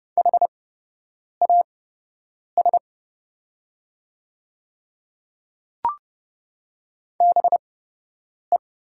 Morse code, heard as a keyed tone, about 30 wpm, spaced as for 8 wpm, 700 Hz. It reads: HAS BE
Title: 8x Character Spacing - Top 100 Words - 30wpm